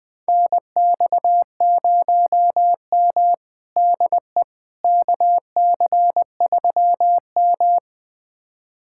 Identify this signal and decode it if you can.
Morse code, heard as a keyed tone, about 20 wpm, 700 Hz. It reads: NX0M DE KC3M
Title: USA Callsign DE Callsign - 4 Characters - 20wpm